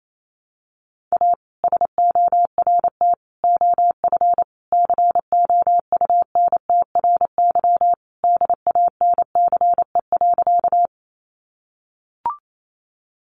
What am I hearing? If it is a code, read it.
Morse code, heard as a keyed tone, about 28 wpm, 700 Hz. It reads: A SORT OF COUNTRY DANCE.
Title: Sentences from Top 1300 Words 28wpm